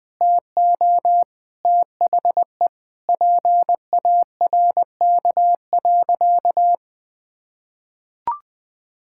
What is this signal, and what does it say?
Morse code, heard as a keyed tone, about 20 wpm, 700 Hz. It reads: TO THE PARK.
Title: Sentences with 4-Word Phrases - 20wpm